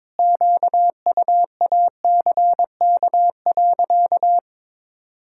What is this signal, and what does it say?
Morse code, heard as a keyed tone, about 22 wpm, 700 Hz. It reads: QUACK.
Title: Pangram Challenge v2 - 22wpm